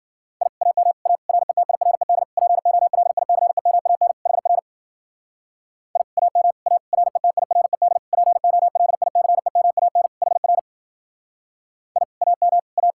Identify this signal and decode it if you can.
Morse code, heard as a keyed tone, about 52 wpm, 700 Hz. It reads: I AM A RETIRED POLICEMAN HR I AM A RETIRED POLICEMAN HR I AM A